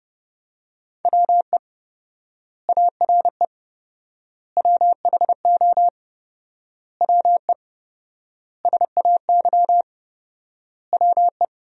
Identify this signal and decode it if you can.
Morse code, heard as a keyed tone, about 30 wpm, 700 Hz. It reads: WE ARE WHO WE SAY WE
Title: 4x Word-Spacing - Sentences from Top 200 Words - 30wpm